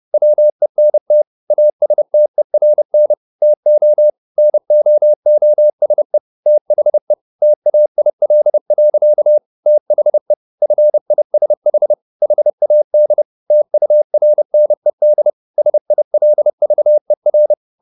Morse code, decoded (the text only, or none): WENT ASTERN TO NOOSE THE TAIL. THE FISH HAD TURNED SILVER